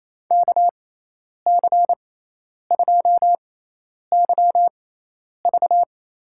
Morse code, heard as a keyed tone, about 27 wpm, spaced as for 10 wpm, 700 Hz. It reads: KC2YV
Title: USA Call Signs - 6x Character Spacing - 28wpm